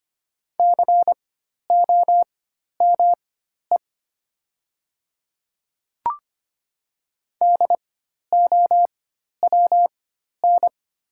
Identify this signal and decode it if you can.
Morse code, heard as a keyed tone, about 25 wpm, spaced as for 12 wpm, 690 Hz. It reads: COME DOWN
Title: 4x Character Spacing - Top 100 Words - 25wpm